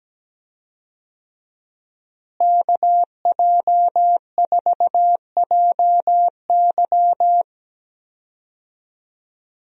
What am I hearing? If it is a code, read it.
Morse code, heard as a keyed tone, about 17 wpm, 700 Hz. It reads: KJ4JY
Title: USA Call Signs - 5 Characters - Repeated 2x - 17wpm